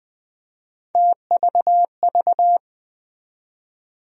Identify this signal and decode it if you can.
Morse code, heard as a keyed tone, about 20 wpm, 700 Hz. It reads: TVV
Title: USA Call Sign Postfixes - 3 Letters 20wpm